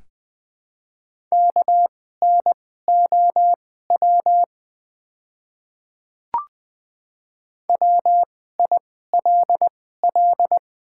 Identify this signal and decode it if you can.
Morse code, heard as a keyed tone, about 20 wpm, spaced as for 14 wpm, 700 Hz. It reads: KNOW WILL
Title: 2x Character Spacing - Top 200 Words - 20wpm